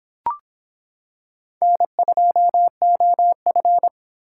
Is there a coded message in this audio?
Yes: Morse code from 1.6 to 3.9 s, heard as a keyed tone, about 25 wpm, 700 Hz. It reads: N2OF